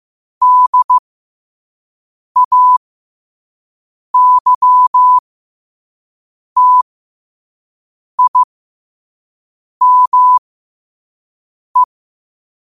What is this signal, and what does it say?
Morse code, heard as a keyed tone, about 15 wpm, spaced as for 5 wpm, 1000 Hz. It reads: DAYTIME
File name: qotdfiles / 5wpm